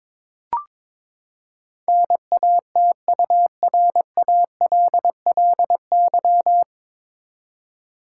Morse code, heard as a keyed tone, about 22 wpm, 700 Hz. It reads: NATURALLY
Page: Morse Code Ninja practice 22wpm